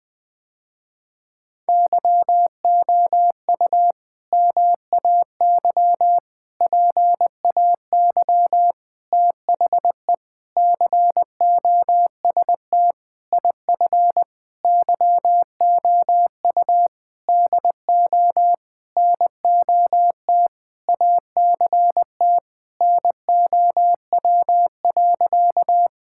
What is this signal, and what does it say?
Morse code, heard as a keyed tone, about 19 wpm, 700 Hz. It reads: YOU MAY PAY THE COST IF YOU DO NOT ACT NOW.
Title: Sentences with 8-Word Phrases - Spoken Beginning - 20wpm